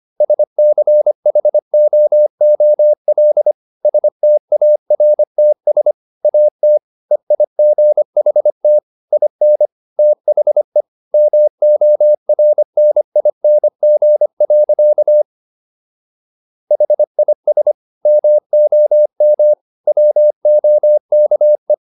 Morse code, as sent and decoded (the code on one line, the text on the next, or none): ... -.-. .... --- --- .-.. / ... - .- .-. - ... / .- - / . .. --. .... - / .. -. / - .... / -- --- .-. -. .. -. --. .-.-.- / .... .. ... / -- --- -- / .-- --- -.- .
SCHOOL STARTS AT EIGHT IN THE MORNING. HIS MOM WOKE